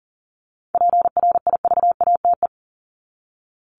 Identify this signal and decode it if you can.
Morse code, heard as a keyed tone, about 40 wpm, 700 Hz. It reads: PRIVATE
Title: Top 751-1000 Words 40wpm